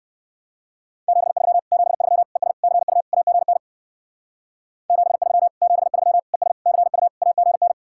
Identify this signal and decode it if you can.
Morse code, heard as a keyed tone, about 58 wpm, 700 Hz. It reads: 73 73 ES CU AGN 73 73 ES CU AGN